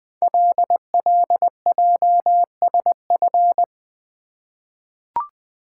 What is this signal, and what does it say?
Morse code, heard as a keyed tone, about 20 wpm, 700 Hz. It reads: LLJSF
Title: Code Groups - 5 Letters - 20wpm